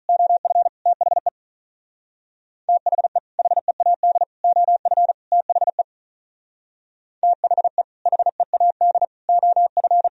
Morse code, heard as a keyed tone, about 44 wpm, 700 Hz. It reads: OF THE THE HEAD OF THE THE HEAD OF